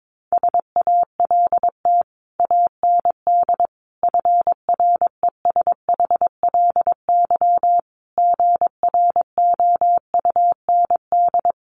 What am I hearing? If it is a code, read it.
Morse code, heard as a keyed tone, about 22 wpm, 700 Hz. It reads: SALT AND FRESHLY GROUND
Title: Sets of 5 Words Encore 22wpm